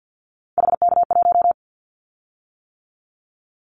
Morse code, heard as a keyed tone, about 50 wpm, 700 Hz. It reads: HK1
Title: Callsign Prefixes for Top 100 DXCC - 50wpm